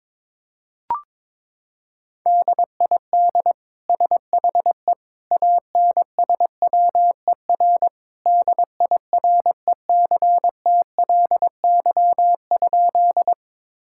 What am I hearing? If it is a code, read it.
Morse code, heard as a keyed tone, about 22 wpm, 700 Hz. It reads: DID SHE ANSWER DIRECTLY?
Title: Sentences from Top 1000 Words - Review Part 5 - 22wpm